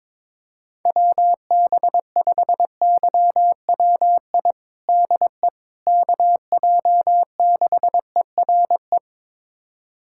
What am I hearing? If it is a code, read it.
Morse code, heard as a keyed tone, about 22 wpm, 700 Hz. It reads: WB5YWI DE KJ6ERE
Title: USA Callsign DE Callsign - 6 Characters - 22wpm